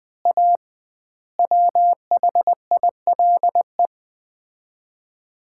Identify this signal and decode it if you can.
Morse code, heard as a keyed tone, about 21 wpm, 700 Hz. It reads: A WHILE
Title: 2x Word-Spacing - Sets of 5 Words - 20wpm